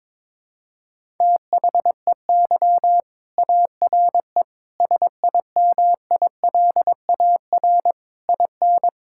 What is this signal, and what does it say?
Morse code, heard as a keyed tone, about 22 wpm, 700 Hz. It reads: THEY ARE SIMILAR IN
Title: Sentences from Top 1000 Words - Review Part 5 - 22wpm